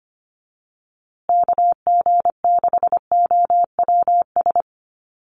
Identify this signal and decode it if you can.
Morse code, heard as a keyed tone, about 25 wpm, 700 Hz. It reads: KG6OWS
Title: USA Call Signs Repeated 3x 25wpm